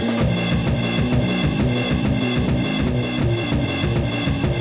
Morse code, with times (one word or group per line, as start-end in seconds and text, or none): none